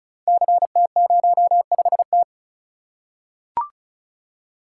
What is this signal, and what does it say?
Morse code, heard as a keyed tone, about 36 wpm, 690 Hz. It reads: CT05T
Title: Code Groups - 5 Letter:Numbers - 35wpm